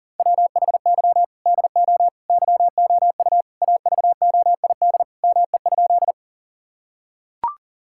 Morse code, heard as a keyed tone, about 40 wpm, 700 Hz. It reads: WHY DO YOU AVOID ME?
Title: Sentences from Top 1000 Words - Review Part 6 - 40wpm